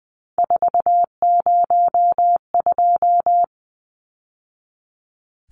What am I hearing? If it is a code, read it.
Morse code, heard as a keyed tone, about 20 wpm, 700 Hz. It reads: 402
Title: Numbers - 3-Digit 20wpm